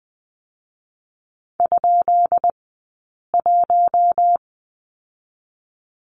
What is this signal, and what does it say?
Morse code, heard as a keyed tone, about 20 wpm, 700 Hz. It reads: ? 1
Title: 132 Lesson 22d - Question Mark - Two Characters 20wpm